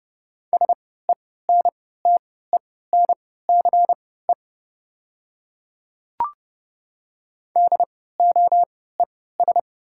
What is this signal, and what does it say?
Morse code, heard as a keyed tone, about 29 wpm, spaced as for 17 wpm, 700 Hz. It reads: SENTENCE DOES
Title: 3x Character Spacing - Top 200 Words - 30wpm